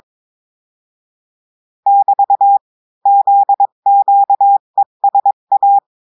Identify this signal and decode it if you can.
Morse code, heard as a keyed tone, about 22 wpm, 800 Hz. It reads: <BT> ZQESA
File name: cours-du-mercredi / audios / cours-853-du-2-mai-2024-etxte chiffre et texte clair-a-20-wpm